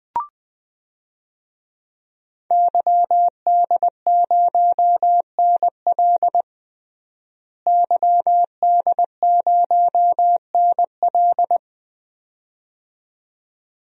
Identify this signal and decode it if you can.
Morse code, heard as a keyed tone, about 20 wpm, 700 Hz. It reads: YD0NL YD0NL
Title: DXCC-100 Call Signs - 5 Characters - Repeated 2x - 20wpm